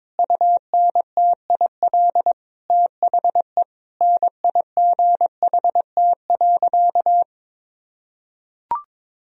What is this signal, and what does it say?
Morse code, heard as a keyed tone, about 22 wpm, 700 Hz. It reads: UNTIL THE NIGHT.
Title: Sentences from Top 1000 Words - Review Part 1 - 22wpm